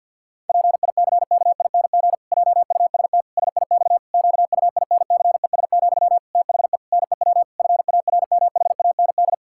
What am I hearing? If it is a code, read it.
Morse code, heard as a keyed tone, about 50 wpm, 700 Hz. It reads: PICKING JUST SIX QUINCES, THE NEW FARMHAND
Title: Pangram Challenge v2 - 50wpm